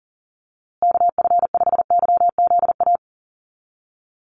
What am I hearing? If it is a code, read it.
Morse code, heard as a keyed tone, about 40 wpm, 700 Hz. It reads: KF5YZA
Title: USA Call Signs Repeated 3x 40wpm